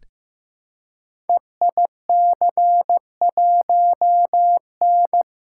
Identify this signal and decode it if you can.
Morse code, heard as a keyed tone, about 15 wpm, 700 Hz. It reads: EIC1N
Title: Code Groups - 5 Letter:Numbers - 15wpm